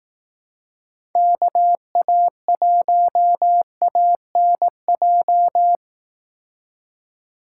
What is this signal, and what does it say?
Morse code, heard as a keyed tone, about 18 wpm, 700 Hz. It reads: KA1ANJ
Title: Speed Racing - USA 2x3 Call Signs 15wpm